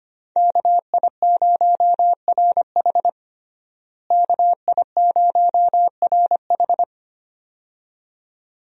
Morse code, heard as a keyed tone, about 25 wpm, 700 Hz. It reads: KI0RH KI0RH
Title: USA Call Signs - 5 Characters - Repeated 2x - 25wpm